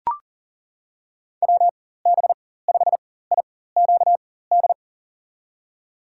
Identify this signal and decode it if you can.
Morse code, heard as a keyed tone, about 40 wpm, spaced as for 19 wpm, 700 Hz. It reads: WB5IQD
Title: USA Call Signs - 4x Character Spacing - 40wpm